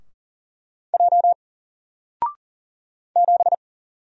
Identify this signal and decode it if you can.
Morse code, heard as a keyed tone, about 40 wpm, 700 Hz. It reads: J 7
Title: Single Character (Letter:Number:Punctuation:Prosign) Alt 40wpm